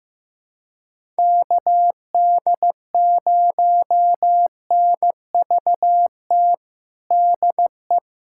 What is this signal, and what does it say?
Morse code, heard as a keyed tone, about 15 wpm, 700 Hz. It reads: KD0NVT DE
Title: USA Callsign DE Callsign 15wpm